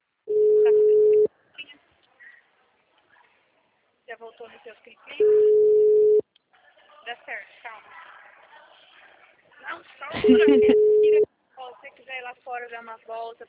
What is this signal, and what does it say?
Ringback tone (Europe, 425 Hz), with speech.
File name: out- -2002- - - .26697